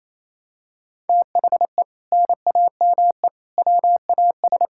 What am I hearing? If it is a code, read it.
Morse code, heard as a keyed tone, about 28 wpm, 700 Hz. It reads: THE NAME WAS
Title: Sentences from Top 1600 Words 28wpm